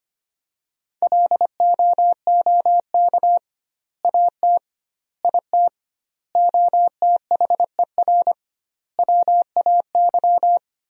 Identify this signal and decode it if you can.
Morse code, heard as a keyed tone, about 25 wpm, 700 Hz. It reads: LOOK AT IT OTHER WAY
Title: 2x Word-Spacing - Sentences from Top 200 Words - 25wpm